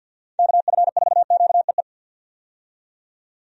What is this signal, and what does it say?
Morse code, heard as a keyed tone, about 53 wpm, 700 Hz. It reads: KF4QEE
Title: USA Call Signs - 6 Characters - 50wpm